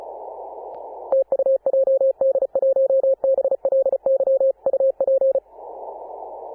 Morse code, heard as a keyed tone, about 34 wpm, 540 Hz. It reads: TU JD1BLY UP
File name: Apr/29 2042z JD1BLY 14.011MHz CW